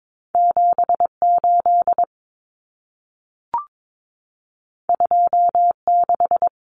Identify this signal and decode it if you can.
Morse code, heard as a keyed tone, about 22 wpm, 700 Hz. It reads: 78 26
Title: Common Cut Numbers - 2-Digit 22wpm